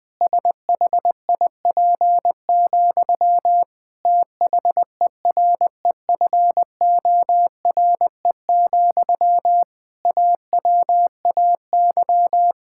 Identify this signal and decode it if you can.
Morse code, heard as a keyed tone, about 20 wpm, 700 Hz. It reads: SHIP, THEREFORE, AWAY
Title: Sentences from Top 1000 Words - Review Part 5 - 20wpm